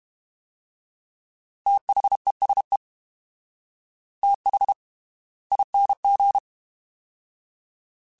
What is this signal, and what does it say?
Morse code, heard as a keyed tone, about 32 wpm, spaced as for 25 wpm, 800 Hz. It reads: THESE THING
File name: TH / TH mp3s / Fast TH @30